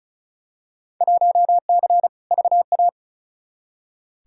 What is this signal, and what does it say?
Morse code, heard as a keyed tone, about 36 wpm, 690 Hz. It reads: 1C VA
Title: Field Day Exchanges Encore - 35wpm